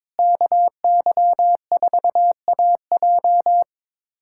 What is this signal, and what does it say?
Morse code, heard as a keyed tone, about 21 wpm, 700 Hz. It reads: KY4AJ
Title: USA Callsign DE Callsign - 5 Characters - 22wpm